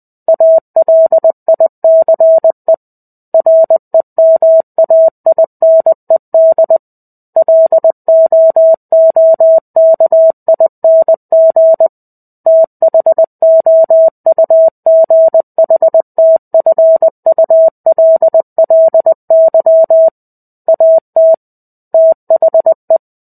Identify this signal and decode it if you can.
Morse code, heard as a keyed tone, about 20 wpm, 650 Hz. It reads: ALICE REMAINED LOOKING THOUGHTFULLY AT THE
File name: 05-Chapter-05 20 WPM